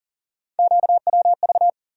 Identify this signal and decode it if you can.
Morse code, heard as a keyed tone, about 40 wpm, 700 Hz. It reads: QWV
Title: USA Call Sign Postfixes - 3 Letters 40wpm